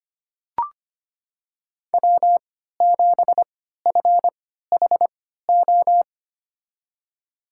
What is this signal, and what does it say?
Morse code, heard as a keyed tone, about 25 wpm, spaced as for 14 wpm, 700 Hz. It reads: W7FHO
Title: USA Call Signs - 5 Characters - 3x Character Spacing - 25wpm